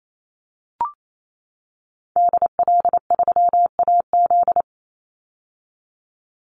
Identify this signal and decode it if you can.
Morse code, heard as a keyed tone, about 28 wpm, 700 Hz. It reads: DL3AZ
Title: Call Signs - 28wpm